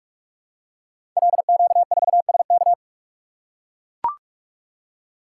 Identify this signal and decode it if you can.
Morse code, heard as a keyed tone, about 44 wpm, 700 Hz. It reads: LQ4SK
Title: Code Groups - 5 Letter:Numbers - 45wpm